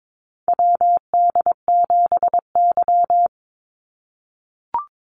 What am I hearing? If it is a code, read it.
Morse code, heard as a keyed tone, about 22 wpm, 700 Hz. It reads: WD7Y
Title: NA SOTA Chaser Callsigns 22wpm